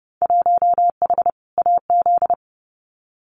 Morse code, heard as a keyed tone, about 30 wpm, 700 Hz. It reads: 1H AZ
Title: Winter Field Day Exchanges 30wpm